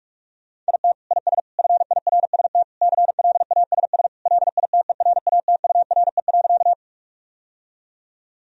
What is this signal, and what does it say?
Morse code, heard as a keyed tone, about 46 wpm, 700 Hz. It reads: IT IS FIRST CLASS LITERATURE.
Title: Sentences from Top 1000 Words - Review Part 3 - 45wpm